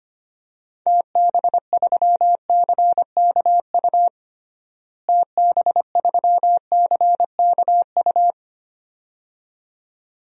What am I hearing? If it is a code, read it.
Morse code, heard as a keyed tone, about 25 wpm, 700 Hz. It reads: TB3CKU TB3CKU
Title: DXCC-100 Call Signs - 6 Characters - Repeated 2x - 25wpm